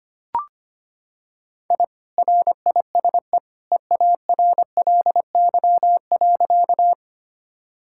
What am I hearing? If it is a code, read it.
Morse code, heard as a keyed tone, about 25 wpm, 700 Hz. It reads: I RISE EARLY.
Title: Sentences from Top 1000 Words - Review Part 6 - 25wpm